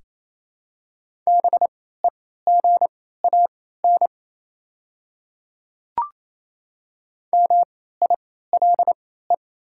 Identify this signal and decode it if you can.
Morse code, heard as a keyed tone, about 28 wpm, spaced as for 16 wpm, 700 Hz. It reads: BEGAN MILE
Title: 3x Character Spacing - Top 300 Words - 28wpm